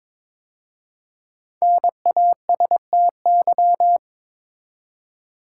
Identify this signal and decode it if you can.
Morse code, heard as a keyed tone, about 22 wpm, 700 Hz. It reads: NASTY
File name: 5 Letter Words - First 3 Letters Given - 22wpm